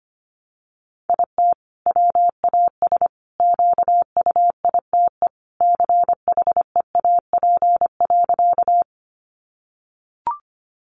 Morse code, heard as a keyed tone, about 25 wpm, 700 Hz. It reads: IT WAS QUITE CHEAP.
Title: Sentences from Top 1200 Words 25wpm